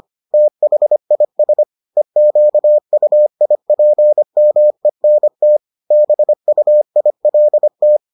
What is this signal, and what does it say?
Morse code, heard as a keyed tone, about 25 wpm, 600 Hz. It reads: THIS EQUIPMENT BUILT